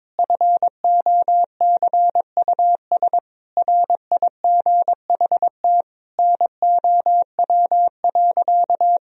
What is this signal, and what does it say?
Morse code, heard as a keyed tone, about 22 wpm, 700 Hz. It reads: FOCUS RIGHT NOW.
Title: Sentences from Top 1000 Words - Review Part 4 - 22wpm